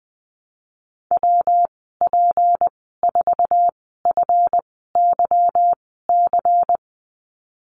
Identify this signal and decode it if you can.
Morse code, heard as a keyed tone, about 20 wpm, spaced as for 15 wpm, 700 Hz. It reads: WP4FYC
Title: 161 Lesson 26g - Letter g - Call Signs 20wpm